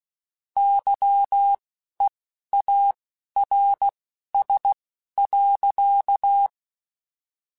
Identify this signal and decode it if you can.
Morse code, heard as a keyed tone, about 16 wpm, spaced as for 11 wpm, 800 Hz. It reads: YEARS.
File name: (spaced) / QSOs Spaced-mp3s / kc7haa @15 sp